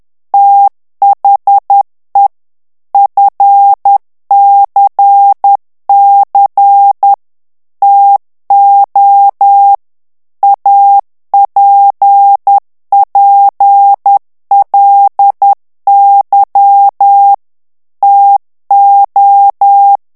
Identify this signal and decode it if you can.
Morse code, heard as a keyed tone, about 10 wpm, 800 Hz. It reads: THE FCC TO APPLY TO
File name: ANGLAIS texte clair 3 vitesse 10 mots minute